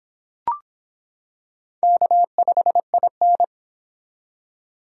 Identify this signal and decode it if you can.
Morse code, heard as a keyed tone, about 24 wpm, 700 Hz. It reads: K5IN